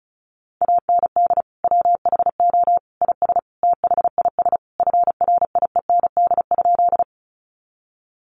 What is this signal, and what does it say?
Morse code, heard as a keyed tone, about 34 wpm, 700 Hz. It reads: AND WHO IS THIS FRIEND?
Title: Sentences from Top 300 Words 35wpm